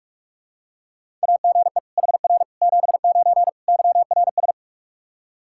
Morse code, heard as a keyed tone, about 46 wpm, 700 Hz. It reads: AGE HR 79 YRS
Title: Practice QSO - Exchange 3 - More Info - 45wpm